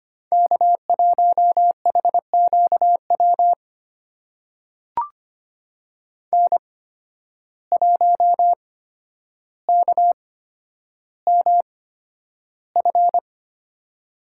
Morse code, heard as a keyed tone, about 25 wpm, spaced as for 11 wpm, 700 Hz. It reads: K1HQW N1KMF
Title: USA Call Signs - 5 Characters - 8x Character Spacing - 25wpm